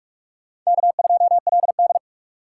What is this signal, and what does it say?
Morse code, heard as a keyed tone, about 46 wpm, 690 Hz. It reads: K2LD
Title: USA Call Signs - 4 Characters - 3x Character Spacing - 45wpm